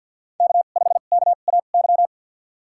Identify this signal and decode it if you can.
Morse code, heard as a keyed tone, about 51 wpm, spaced as for 36 wpm, 700 Hz. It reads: K5KAY
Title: USA Call Signs - 5 Characters - 2x Character Spacing - 50wpm